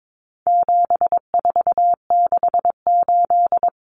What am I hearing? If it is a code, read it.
Morse code, heard as a keyed tone, about 22 wpm, 700 Hz. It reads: 7468
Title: Numbers - 4-Digit 22wpm